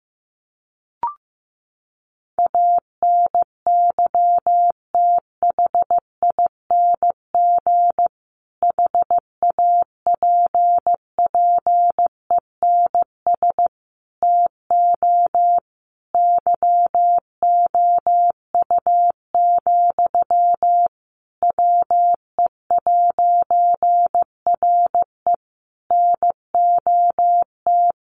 Morse code, heard as a keyed tone, about 15 wpm, 700 Hz. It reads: ANYTHING HAPPENS TO YOU, WE'RE NOT
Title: A Sound Of Thunder 15wpm